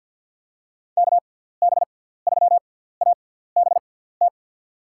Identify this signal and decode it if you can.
Morse code, heard as a keyed tone, about 52 wpm, spaced as for 17 wpm, 700 Hz. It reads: KB3ABT
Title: USA Call Signs - 6 Characters - 6x Character Spacing - 50wpm